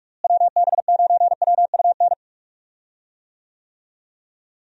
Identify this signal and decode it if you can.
Morse code, heard as a keyed tone, about 46 wpm, 700 Hz. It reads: WB9WUN
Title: USA Call Signs - 6 Characters - Repeated 2x - 45wpm